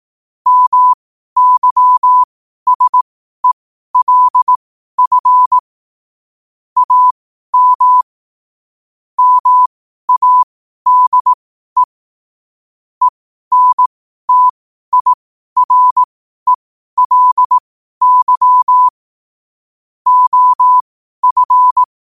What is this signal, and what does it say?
Morse code, heard as a keyed tone, about 18 wpm, spaced as for 12 wpm, 1000 Hz. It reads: MYSELF AM MADE ENTIRELY OF
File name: qotdfiles / 12wpm